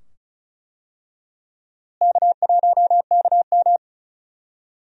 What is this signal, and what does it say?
Morse code, heard as a keyed tone, about 36 wpm, 700 Hz. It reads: K1KM
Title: USA Call Signs - 4 Characters - Repeated 2x - 35wpm